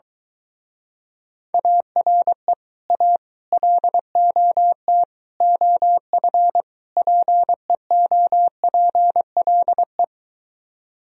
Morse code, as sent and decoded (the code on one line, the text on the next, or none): .- .-. / .- / .-.. --- - / --- ..-. / .--. --- .--. .-.. .
ARE A LOT OF PEOPLE